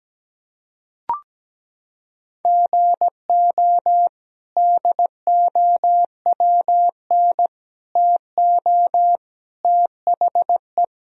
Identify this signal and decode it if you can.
Morse code, heard as a keyed tone, about 17 wpm, 700 Hz. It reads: GO DOWN TO THE
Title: Sentences from Top 1000 Words - Review Part 4 - 17wpm